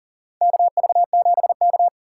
Morse code, heard as a keyed tone, about 40 wpm, 690 Hz. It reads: KV7K
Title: USA Call Signs - 2x Character Spacing - 40wpm